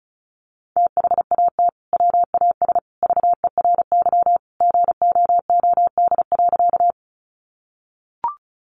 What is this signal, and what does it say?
Morse code, heard as a keyed tone, about 34 wpm, 700 Hz. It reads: THAT WAS VERY GOOD.
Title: Sentences from Top 200 Words 35wpm